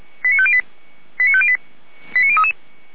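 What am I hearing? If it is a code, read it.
Five-tone selective call (ZVEI), heard as quick runs of five tones, tones of about 70 ms. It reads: 89508 89508 89033
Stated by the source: ZVEI five-tone selective call received on SDR.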